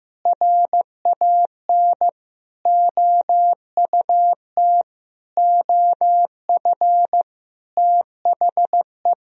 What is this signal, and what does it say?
Morse code, heard as a keyed tone, about 15 wpm, 700 Hz. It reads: RAN OUT OF THE
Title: Sentences from Top 500 Words and Abbreviations 15wpm